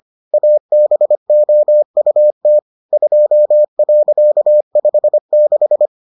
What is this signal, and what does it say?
Morse code, heard as a keyed tone, about 25 wpm, 600 Hz. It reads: ABOUT 2.56